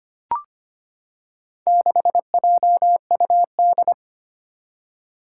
Morse code, heard as a keyed tone, about 25 wpm, 700 Hz. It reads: 6JUD
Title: Four Challenging Letter-Number Combinations 25wpm